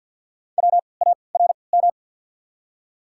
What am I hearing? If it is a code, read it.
Morse code, heard as a keyed tone, about 53 wpm, spaced as for 29 wpm, 700 Hz. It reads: WARM
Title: 3x Character Spacing - Top 500 Words - 50wpm